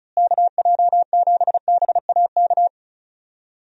Morse code, heard as a keyed tone, about 34 wpm, 700 Hz. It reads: KJ7BAK
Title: USA Call Signs - 6 Characters - Repeated 2x - 35wpm